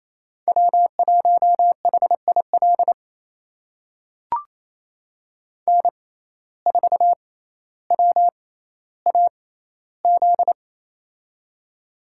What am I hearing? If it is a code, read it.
Morse code, heard as a keyed tone, about 28 wpm, spaced as for 14 wpm, 700 Hz. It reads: W1HIL N4WAZ
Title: USA Call Signs - 5 Characters - 6x Character Spacing - 28wpm